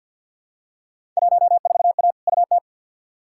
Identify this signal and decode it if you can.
Morse code, heard as a keyed tone, about 53 wpm, 700 Hz. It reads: 14A UT
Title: Field Day Exchanges Encore - 50wpm